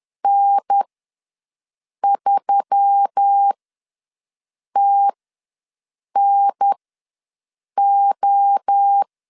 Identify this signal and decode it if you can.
Morse code, heard as a keyed tone, about 10 wpm, spaced as for 6 wpm, 800 Hz. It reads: N3TNO
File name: LEZIONI / 5 LBVC / 5 L ws 10wpm 3636